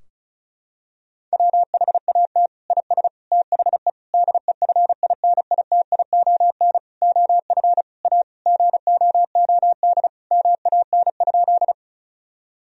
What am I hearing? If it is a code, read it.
Morse code, heard as a keyed tone, about 36 wpm, 700 Hz. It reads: WHAT IS THE DEFINITION OF A GOOD MAN?
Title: Sentences from Top 1000 Words - Review Part 4 - 35wpm